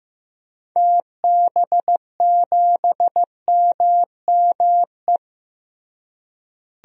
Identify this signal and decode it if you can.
Morse code, heard as a keyed tone, about 16 wpm, 700 Hz. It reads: TB7MME
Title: DXCC-100 Call Signs - 6 Characters - Repeated 3x - 15wpm